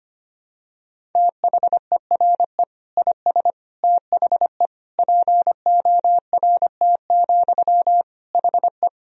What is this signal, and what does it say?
Morse code, heard as a keyed tone, about 25 wpm, 700 Hz. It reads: THERE IS THE PORT, HE
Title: Sentences from Top 1000 Words - Review Part 1 - 25wpm